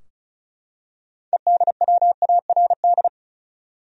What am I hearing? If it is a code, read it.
Morse code, heard as a keyed tone, about 34 wpm, 700 Hz. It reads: EDWARD
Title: USA Common Names - 35wpm